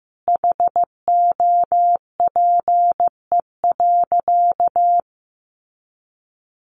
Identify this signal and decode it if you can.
Morse code, heard as a keyed tone, about 15 wpm, 700 Hz. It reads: HOPE.
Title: Sentences from Top 1000 Words 15wpm